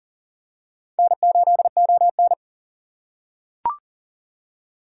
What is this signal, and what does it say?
Morse code, heard as a keyed tone, about 40 wpm, 700 Hz. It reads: N8ON
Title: USA Call Signs - 4 Characters - 4x Character Spacing - 40wpm